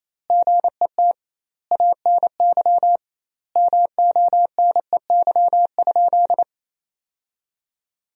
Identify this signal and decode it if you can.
Morse code, heard as a keyed tone, about 28 wpm, 700 Hz. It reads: GET ANY MONEY?
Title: 2x Word-Spacing - Sentences from Top 400 Words - 28wpm